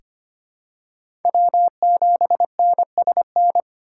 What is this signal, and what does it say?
Morse code, heard as a keyed tone, about 25 wpm, 700 Hz. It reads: W7NSN
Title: USA Call Signs - 5 Characters - 4x Character Spacing - 25wpm